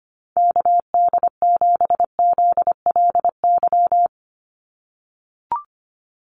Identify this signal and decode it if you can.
Morse code, heard as a keyed tone, about 25 wpm, 700 Hz. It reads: KD7ZLY
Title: USA Call Signs 25wpm